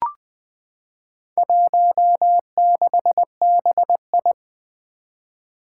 Morse code, heard as a keyed tone, about 21 wpm, 700 Hz. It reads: 16BI
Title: Four Challenging Letter-Number Combinations 20wpm